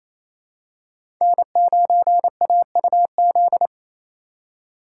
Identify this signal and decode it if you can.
Morse code, heard as a keyed tone, about 28 wpm, 700 Hz. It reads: N9AUZ
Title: USA Call Signs - 4x Character Spacing - 28wpm